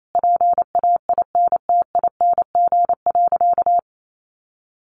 Morse code, heard as a keyed tone, about 28 wpm, 700 Hz. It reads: PAINTING.
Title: Sentences from Top 1200 Words 28wpm